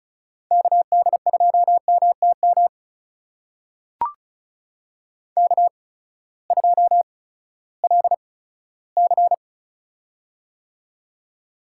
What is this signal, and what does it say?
Morse code, heard as a keyed tone, about 36 wpm, spaced as for 18 wpm, 700 Hz. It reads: KD2MTM K2LC
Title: USA Call Signs - 8x Character Spacing - 35wpm